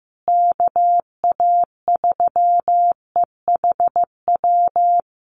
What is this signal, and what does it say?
Morse code, heard as a keyed tone, about 15 wpm, 700 Hz. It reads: KA3EHW
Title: USA Call Signs Repeated 3x 15wpm